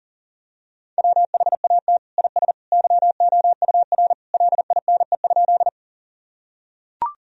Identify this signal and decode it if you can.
Morse code, heard as a keyed tone, about 40 wpm, 700 Hz. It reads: WHAT IS YOUR LINE?
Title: Sentences from Top 1000 Words - Review Part 1 - 40wpm